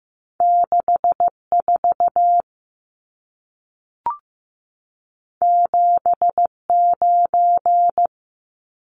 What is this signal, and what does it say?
Morse code, heard as a keyed tone, about 15 wpm, 700 Hz. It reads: 64 79
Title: Numbers - 2-Digit 15wpm